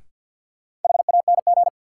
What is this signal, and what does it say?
Morse code, heard as a keyed tone, about 49 wpm, 700 Hz. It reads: HANG
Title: Sentences from Top 1600 Words - New Word Drill - 50wpm